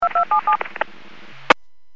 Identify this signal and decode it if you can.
DTMF, heard as telephone touch tones, tones of about 110 ms. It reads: 2277
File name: WPS / 00340 16-45 WPS 451.25 WPS